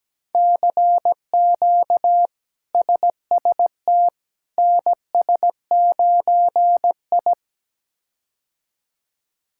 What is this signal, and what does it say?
Morse code, heard as a keyed tone, about 17 wpm, 700 Hz. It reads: CQ SST NS9I
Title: K1USN SST QSO Practice - 17wpm